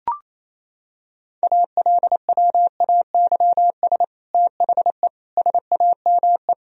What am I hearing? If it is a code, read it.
Morse code, heard as a keyed tone, about 28 wpm, 700 Hz. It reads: ALWAYS THE SAME
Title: Sentences from Top 1000 Words - Review Part 1 - 28wpm